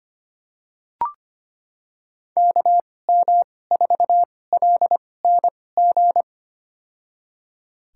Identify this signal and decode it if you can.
Morse code, heard as a keyed tone, about 25 wpm, spaced as for 18 wpm, 700 Hz. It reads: KM4LNG
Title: USA Call Signs - 6 Characters - 2x Character Spacing - 25wpm